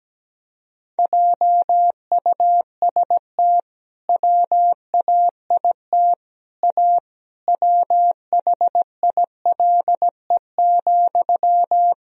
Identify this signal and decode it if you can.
Morse code, heard as a keyed tone, about 17 wpm, 700 Hz. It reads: JUST WAIT A WHILE,
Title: Sentences from Top 1000 Words - Review Part 1 - 17wpm